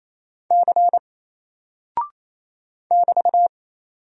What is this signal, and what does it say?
Morse code, heard as a keyed tone, about 28 wpm, 700 Hz. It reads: C <BT>
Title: Single Character (Letter:Number:Punctuation:Prosign) Alt 28wpm